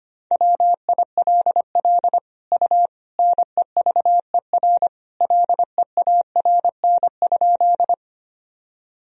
Morse code, heard as a keyed tone, about 25 wpm, 700 Hz. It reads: WILL U NEVER LEARN?
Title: Sentences from Top 300 Words and Abbreviations 25wpm